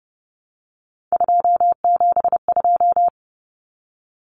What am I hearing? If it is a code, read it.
Morse code, heard as a keyed tone, about 30 wpm, 700 Hz. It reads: 272
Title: Numbers - 3-Digit 30wpm